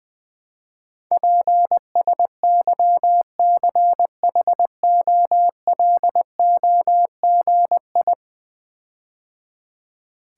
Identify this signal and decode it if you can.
Morse code, heard as a keyed tone, about 20 wpm, 700 Hz. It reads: PSYCHOLOGI
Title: Build A Word - 5 Letters At A Time - 12-Letter Words - 20wpm